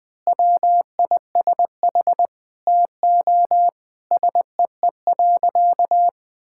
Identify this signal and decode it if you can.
Morse code, heard as a keyed tone, about 20 wpm, 700 Hz. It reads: WISH TO SEE.
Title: Sentences with 7-Word Phrases - 20wpm